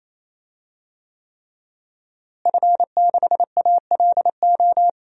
Morse code, heard as a keyed tone, about 28 wpm, 700 Hz. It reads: F6ALO
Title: DXCC-100 Call Signs - 5 Characters - Repeated 3x - 28wpm